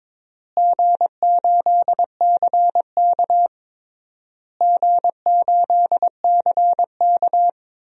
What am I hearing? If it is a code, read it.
Morse code, heard as a keyed tone, about 22 wpm, 700 Hz. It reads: G8CK G8CK
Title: DXCC-100 Call Signs - 4 Characters - Repeated 3x - 22wpm